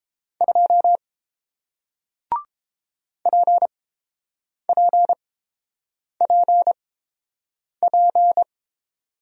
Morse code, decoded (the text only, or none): 2 PPPP